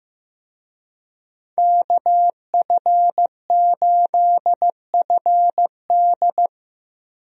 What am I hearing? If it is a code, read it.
Morse code, heard as a keyed tone, about 15 wpm, 700 Hz. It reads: KF8FD
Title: USA Call Signs - 5 Characters - Repeated 3x - 15wpm